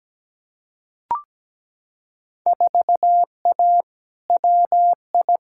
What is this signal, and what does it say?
Morse code, heard as a keyed tone, about 17 wpm, 700 Hz. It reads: 4A WI
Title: Field Day Exchanges - 17wpm